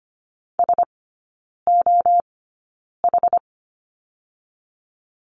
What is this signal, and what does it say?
Morse code, heard as a keyed tone, about 25 wpm, spaced as for 9 wpm, 700 Hz. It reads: SOH
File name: Lesson 15e - Number 5 - Three Characters 25wpm